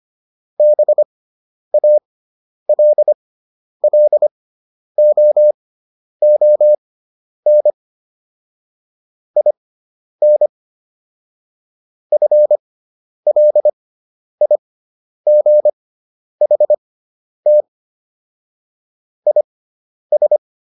Morse code, heard as a keyed tone, about 25 wpm, spaced as for 10 wpm, 600 Hz. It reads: BALLOON IN FLIGHT IS